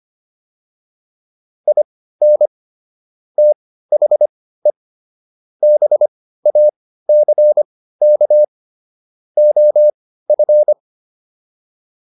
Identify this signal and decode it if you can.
Morse code, heard as a keyed tone, about 25 wpm, spaced as for 15 wpm, 600 Hz. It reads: IN THE BACK OF